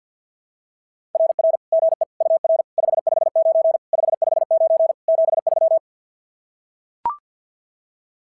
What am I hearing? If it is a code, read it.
Morse code, heard as a keyed tone, about 46 wpm, 660 Hz. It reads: RR GE UR 559 559 73
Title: SOTA QSO Practice 50wpm